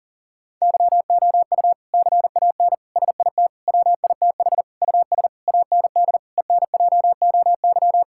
Morse code, heard as a keyed tone, about 40 wpm, 700 Hz. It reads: YOU CAN SIT WITH US AND ENJOY
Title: Sentences from Top 1000 Words - Review Part 5 - 40wpm